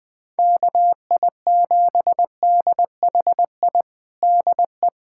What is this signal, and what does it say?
Morse code, heard as a keyed tone, about 21 wpm, 700 Hz. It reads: KI7DHI DE
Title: USA Callsign DE Callsign - 6 Characters - 20wpm